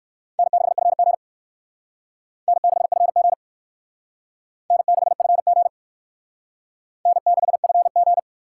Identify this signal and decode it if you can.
Morse code, heard as a keyed tone, about 59 wpm, 700 Hz. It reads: N6FG N6FG N6FG N6FG